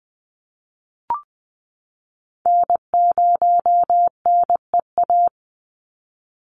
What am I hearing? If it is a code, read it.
Morse code, heard as a keyed tone, about 20 wpm, 700 Hz. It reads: N0NEA
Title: USA Call Signs 20wpm